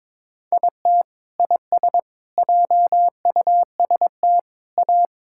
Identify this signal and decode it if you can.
Morse code, heard as a keyed tone, about 21 wpm, 700 Hz. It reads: IT IS JUST A
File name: practice / 22wpm / Sentences with 2-Word Phrases - 22wpm